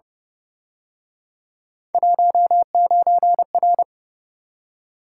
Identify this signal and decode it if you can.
Morse code, heard as a keyed tone, about 30 wpm, 700 Hz. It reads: 19R